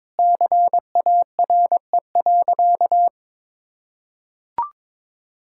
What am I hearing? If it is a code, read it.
Morse code, heard as a keyed tone, about 22 wpm, 700 Hz. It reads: CARE.
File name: practice / 22wpm / Sentences with 4-Word Phrases - 22wpm